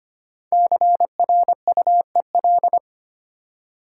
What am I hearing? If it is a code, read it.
Morse code, heard as a keyed tone, about 25 wpm, 700 Hz. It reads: CRUEL
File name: 5 Letter Words - First 3 Letters Given - 25wpm